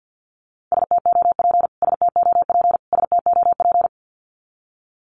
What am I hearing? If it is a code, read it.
Morse code, heard as a keyed tone, about 50 wpm, 700 Hz. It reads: STOP STOP STOP
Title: Top 500 Words Repeated 50wpm